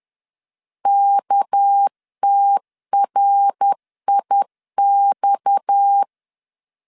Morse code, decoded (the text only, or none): KTRIX